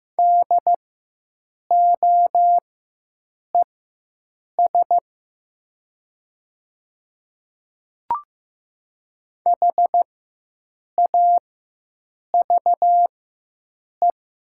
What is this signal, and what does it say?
Morse code, heard as a keyed tone, about 15 wpm, spaced as for 7 wpm, 700 Hz. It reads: DOES HAVE
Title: 4x Character Spacing - Top 300 Words - 15wpm